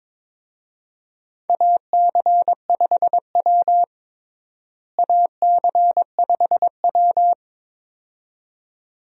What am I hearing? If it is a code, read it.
Morse code, heard as a keyed tone, about 22 wpm, 700 Hz. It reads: AC5W AC5W
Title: USA Call Signs - 4 Characters - Repeated 2x - 22wpm